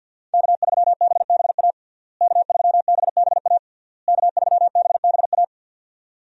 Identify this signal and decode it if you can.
Morse code, heard as a keyed tone, about 49 wpm, 700 Hz. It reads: K3BBA K3BBA K3BBA
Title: USA Call Signs - 5 Characters - Repeated 3x - 50wpm